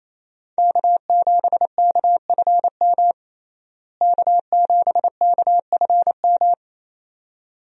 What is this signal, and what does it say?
Morse code, heard as a keyed tone, about 28 wpm, 700 Hz. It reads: K7KFM K7KFM
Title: USA Call Signs - 5 Characters - Repeated 2x - 28wpm